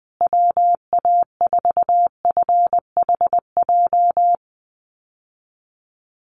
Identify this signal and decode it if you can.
Morse code, heard as a keyed tone, about 20 wpm, 700 Hz. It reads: WA4FHJ
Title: USA Call Signs Repeated 2x 20wpm